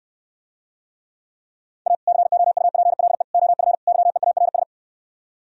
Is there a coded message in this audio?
Yes: Morse code from 1.9 to 4.6 s, heard as a keyed tone, about 70 wpm, 700 Hz. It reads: A COUPLE OF YEARS